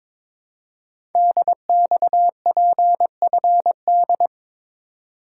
Morse code, heard as a keyed tone, about 22 wpm, 700 Hz. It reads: DXPFD
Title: Code Groups - 5 Letters - 22wpm